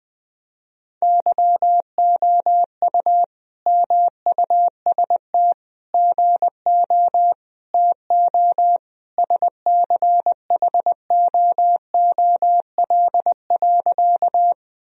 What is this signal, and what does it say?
Morse code, heard as a keyed tone, about 20 wpm, 700 Hz. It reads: YOU MUST GO TO SCHOOL.
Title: Sentences from Top 1000 Words - Review Part 1 - 20wpm